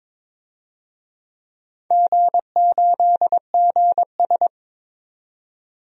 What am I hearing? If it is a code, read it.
Morse code, heard as a keyed tone, about 22 wpm, 700 Hz. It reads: G8GS
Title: DXCC-100 Call Signs - 4 Characters - 22wpm